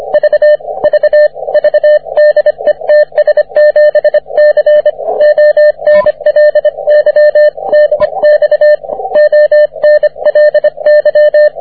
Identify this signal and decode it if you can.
Morse code, heard as a keyed tone, about 25 wpm, 600 Hz, with radio noise. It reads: V V V DE TS7C ONLY DX ONLY
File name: wp / ts7c 30m cw